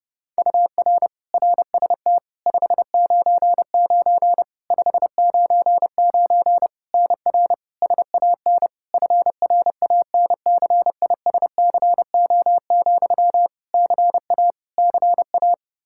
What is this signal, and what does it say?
Morse code, heard as a keyed tone, about 30 wpm, 700 Hz. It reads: UR RST 599 599 NR SAN FRANCISCO, CA CA
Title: Practice QSO - Exchange 1 - Beginning - 30wpm